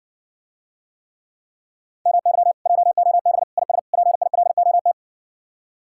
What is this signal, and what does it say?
Morse code, heard as a keyed tone, about 60 wpm, 700 Hz. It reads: MY JOB IS PILOT